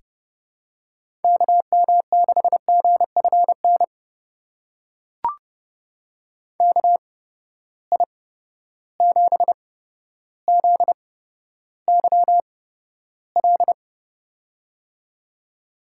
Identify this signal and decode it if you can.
Morse code, heard as a keyed tone, about 30 wpm, spaced as for 13 wpm, 700 Hz. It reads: KM6GFN KI7ZYL
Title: USA Call Signs - 6 Characters - 8x Character Spacing - 30wpm